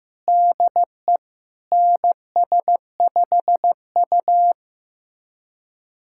Morse code, heard as a keyed tone, about 16 wpm, 700 Hz. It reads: DE NS5U
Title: USA Callsign DE Callsign - 4 Characters - 15wpm